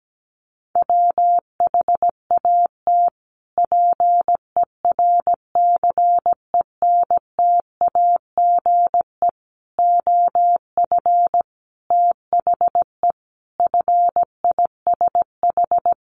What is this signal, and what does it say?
Morse code, heard as a keyed tone, about 17 wpm, 700 Hz. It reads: WHAT PERCENTAGE OF THE FISH
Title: Sentences from Top 1300 Words 17wpm